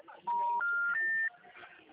Special information tone.